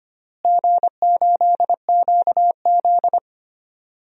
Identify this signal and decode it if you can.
Morse code, heard as a keyed tone, about 25 wpm, 700 Hz. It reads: G8QZ
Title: DXCC-100 Call Signs - 4 Characters - Repeated 3x - 25wpm